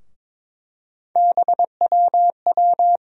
Morse code, heard as a keyed tone, about 22 wpm, 700 Hz. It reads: BWW
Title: USA Call Sign Postfixes - 3 Letters 22wpm